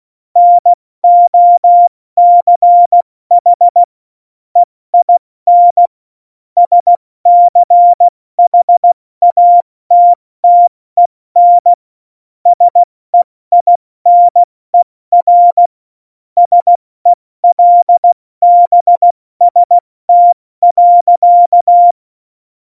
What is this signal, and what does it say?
Morse code, heard as a keyed tone, about 16 wpm, 700 Hz. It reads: NOCH EIN SCHATTEN SEINER SELBST.
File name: Morsekurs / Witze / Witz 29. Dezember 2020, 70 F80